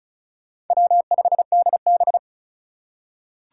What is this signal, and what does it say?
Morse code, heard as a keyed tone, about 36 wpm, 700 Hz. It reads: W5DB
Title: USA Call Signs - 4 Characters - 4x Character Spacing - 35wpm